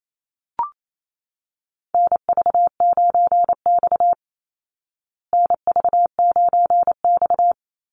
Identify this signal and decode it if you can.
Morse code, heard as a keyed tone, about 28 wpm, 690 Hz. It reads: NV9X NV9X
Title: Call Signs Repeated 3x - 28wpm